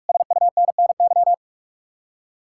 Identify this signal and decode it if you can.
Morse code, heard as a keyed tone, about 59 wpm, spaced as for 52 wpm, 700 Hz. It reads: SUNNY
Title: Speed-Racing Common QSO Phrases - 45wpm